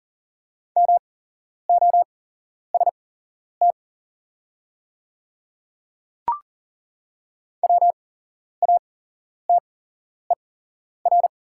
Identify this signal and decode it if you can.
Morse code, heard as a keyed tone, about 38 wpm, spaced as for 11 wpm, 690 Hz. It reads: MOST WATER
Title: 8x Character Spacing - Top 300 Words - 40wpm